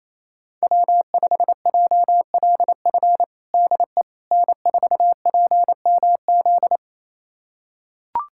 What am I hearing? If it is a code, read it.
Morse code, heard as a keyed tone, about 28 wpm, 700 Hz. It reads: W5JLF DE N4PMZ
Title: USA Callsign DE Callsign - 5 Characters - 28wpm